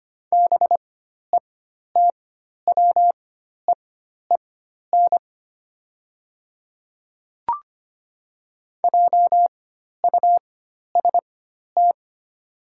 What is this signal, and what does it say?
Morse code, heard as a keyed tone, about 25 wpm, spaced as for 12 wpm, 700 Hz. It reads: BETWEEN JUST
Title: 4x Character Spacing - Top 500 Words - 25wpm